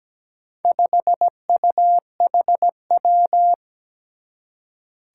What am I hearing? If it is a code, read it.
Morse code, heard as a keyed tone, about 17 wpm, 700 Hz. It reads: 5UHW
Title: Four Challenging Letter-Number Combinations 17wpm